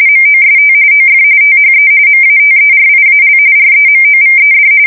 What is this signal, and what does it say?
RTTY, about 45.45 baud. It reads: MOST OF THE MATERIAL REMOVED
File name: fsk170 / USS-Massachusetts